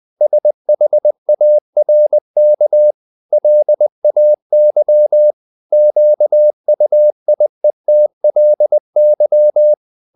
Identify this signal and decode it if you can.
Morse code, heard as a keyed tone, about 20 wpm, 600 Hz. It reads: SHARK LAY QUIETLY
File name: the old man and the sea part 2 20wpm